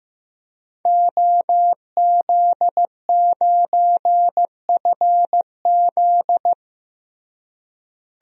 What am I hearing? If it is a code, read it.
Morse code, heard as a keyed tone, about 15 wpm, 700 Hz. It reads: OZ9FZ
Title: DXCC-100 Call Signs - 5 Characters - Repeated 3x - 15wpm